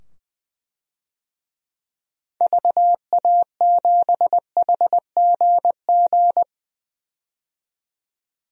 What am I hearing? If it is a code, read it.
Morse code, heard as a keyed tone, about 21 wpm, 700 Hz. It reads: VA7HGG
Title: DXCC-100 Call Signs - 6 Characters - 20wpm